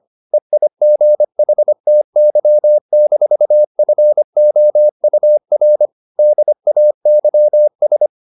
Morse code, heard as a keyed tone, about 25 wpm, 600 Hz. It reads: EIGHTY-FOUR DAYS